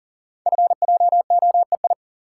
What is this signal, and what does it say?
Morse code, heard as a keyed tone, about 38 wpm, 700 Hz. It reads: FJOEI
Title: Code Groups - 5 Letters - 40wpm